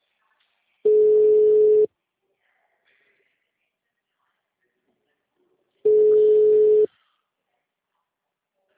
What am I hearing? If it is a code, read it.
Ringback tone (Europe, 425 Hz).